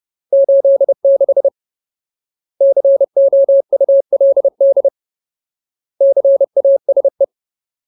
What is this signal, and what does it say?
Morse code, heard as a keyed tone, about 30 wpm, 550 Hz. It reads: 86 COULD CASE
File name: Ultimate Word List / 30WPM Ultimate